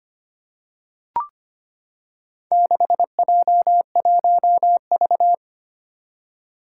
Morse code, heard as a keyed tone, about 25 wpm, 700 Hz. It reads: 6J1V
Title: Four Challenging Letter-Number Combinations 25wpm